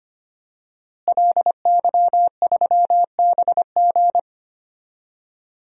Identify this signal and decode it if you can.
Morse code, heard as a keyed tone, about 25 wpm, 700 Hz. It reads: LY3BG
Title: DXCC-100 Call Signs - 5 Characters - Repeated 2x - 25wpm